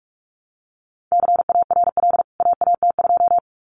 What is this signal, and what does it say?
Morse code, heard as a keyed tone, about 44 wpm, 700 Hz. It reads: CARL AAT2
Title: CWT Exchange 45wpm